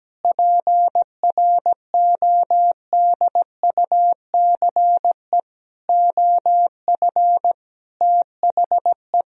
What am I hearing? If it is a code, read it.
Morse code, heard as a keyed tone, about 17 wpm, 700 Hz. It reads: PRODUCE OF THE
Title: Sentences from Top 1000 Words - Review Part 3 - 17wpm